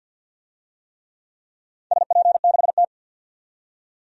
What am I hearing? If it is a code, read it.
Morse code, heard as a keyed tone, about 47 wpm, 700 Hz. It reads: SP6T
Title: DXCC-100 Call Signs - 4 Characters - 50wpm